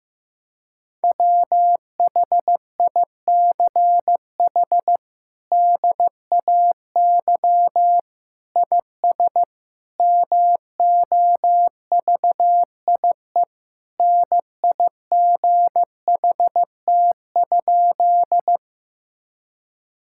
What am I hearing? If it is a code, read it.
Morse code, heard as a keyed tone, about 15 wpm, 700 Hz. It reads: WHICH DAY IS MOVIE NIGHT?
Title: Sentences from Top 1000 Words - Review Part 5 - 15wpm